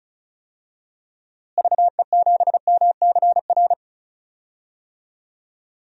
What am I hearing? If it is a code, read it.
Morse code, heard as a keyed tone, about 36 wpm, 700 Hz. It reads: VE7MCR
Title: DXCC-100 Call Signs - 6 Characters - Repeated 3x - 35wpm